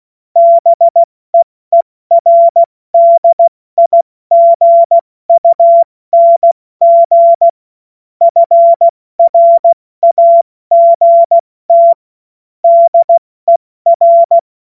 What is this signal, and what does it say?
Morse code, heard as a keyed tone, about 16 wpm, 680 Hz. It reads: BEERDIGUNG FRAGT DER